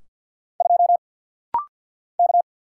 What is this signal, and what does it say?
Morse code, heard as a keyed tone, about 48 wpm, 700 Hz. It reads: 2 K
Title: Single Letter-Number - Rapid-Fire 50wpm